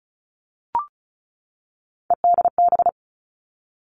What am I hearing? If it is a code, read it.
Morse code, heard as a keyed tone, about 36 wpm, 700 Hz. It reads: EDB
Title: Three Challenging Letter-Number Combinations 35wpm